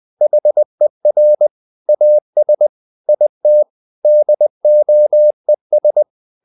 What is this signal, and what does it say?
Morse code, heard as a keyed tone, about 20 wpm, 600 Hz. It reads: HER AS IT DOES
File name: the old man and the sea / the old man and the sea part 1 20wpm